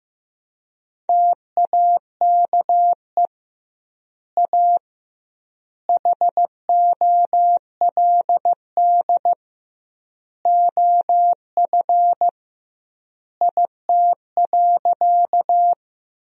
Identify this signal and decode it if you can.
Morse code, heard as a keyed tone, about 15 wpm, 700 Hz. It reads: TAKE A HOLD OF IT.
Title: 2x Word-Spacing - Sentences from Top 400 Words - 15wpm